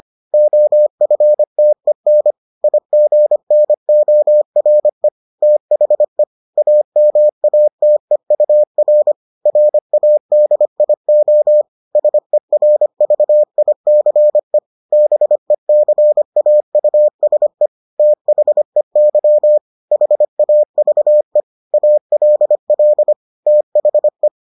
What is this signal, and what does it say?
Morse code, heard as a keyed tone, about 25 wpm, 600 Hz. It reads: OFTEN IGNORE THE AMATEUR RADIO SERVICE BECAUSE THEY HAVE ALL THE